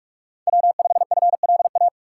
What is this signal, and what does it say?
Morse code, heard as a keyed tone, about 42 wpm, 700 Hz. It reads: W5FLA